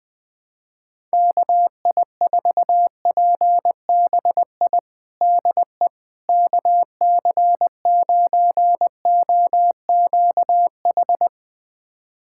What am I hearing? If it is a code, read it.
Morse code, heard as a keyed tone, about 21 wpm, 700 Hz. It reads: KI4PBI DE KC9OQH
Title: USA Callsign DE Callsign 20wpm